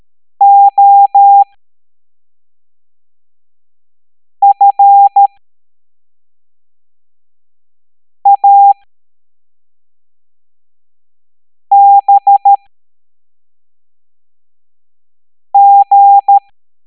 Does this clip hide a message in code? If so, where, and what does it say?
Yes: Morse code from 0.4 to 16.4 s, heard as a keyed tone, about 13 wpm, spaced as for 3 wpm, 800 Hz. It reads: OFABG